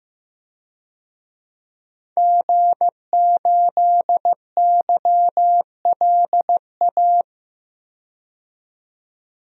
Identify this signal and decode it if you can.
Morse code, heard as a keyed tone, about 15 wpm, 700 Hz. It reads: G8YLA
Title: DXCC-100 Call Signs - 5 Characters - 15wpm